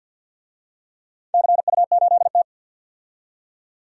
Morse code, heard as a keyed tone, about 47 wpm, 700 Hz. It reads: CU8T
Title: DXCC-100 Call Signs - 4 Characters - 50wpm